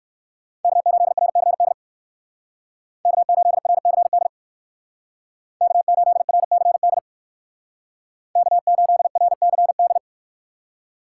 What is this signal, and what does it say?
Morse code, heard as a keyed tone, about 59 wpm, 700 Hz. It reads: K8RCD K8RCD K8RCD K8RCD